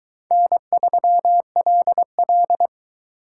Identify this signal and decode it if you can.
Morse code, heard as a keyed tone, about 21 wpm, 700 Hz. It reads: N3LL